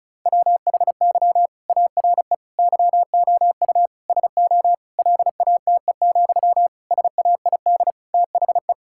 Morse code, heard as a keyed tone, about 36 wpm, 700 Hz. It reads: WHY ARE YOU SO LATE, SAID THE
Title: Sentences from Top 1000 Words - Review Part 6 - 35wpm